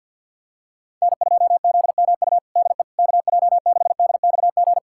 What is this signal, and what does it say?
Morse code, heard as a keyed tone, about 52 wpm, 700 Hz. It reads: N2ZMU DE KJ6DXG
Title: USA Callsign DE Callsign 50wpm